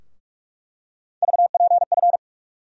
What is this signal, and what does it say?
Morse code, heard as a keyed tone, about 46 wpm, 700 Hz. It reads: VPF
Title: USA Call Sign Postfixes - 3 Letters 45wpm